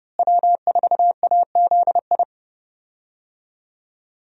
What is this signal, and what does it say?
Morse code, heard as a keyed tone, about 30 wpm, 700 Hz. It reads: W4AZI
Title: USA Call Signs - 5 Characters - Repeated 3x - 30wpm